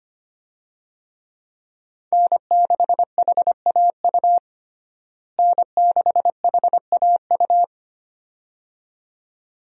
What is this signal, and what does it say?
Morse code, heard as a keyed tone, about 25 wpm, 700 Hz. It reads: N6HAU N6HAU
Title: USA Call Signs - 5 Characters - Repeated 2x - 25wpm